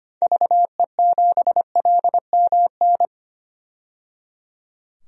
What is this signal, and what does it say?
Morse code, heard as a keyed tone, about 25 wpm, 700 Hz. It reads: VE7LMN
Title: DXCC-100 Call Signs - 6 Characters - 25wpm